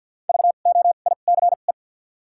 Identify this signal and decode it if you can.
Morse code, heard as a keyed tone, about 49 wpm, 700 Hz. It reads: V O I C E
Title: 2x Character Spacing - Top 500 Words - 50wpm